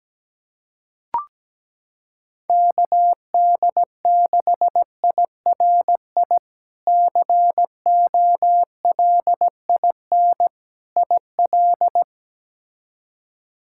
Morse code, heard as a keyed tone, about 17 wpm, 700 Hz. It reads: KD6IRI COLIN IL
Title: K1USN SST QSO Practice - 17wpm